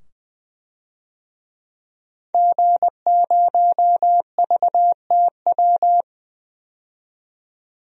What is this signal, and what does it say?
Morse code, heard as a keyed tone, about 20 wpm, 700 Hz. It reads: G0VTW
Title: DXCC-100 Call Signs - 5 Characters - 20wpm